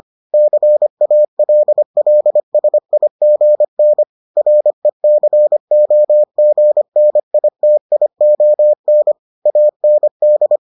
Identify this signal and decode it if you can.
Morse code, heard as a keyed tone, about 25 wpm, 600 Hz. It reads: CALLSIGN RECOGNITION AND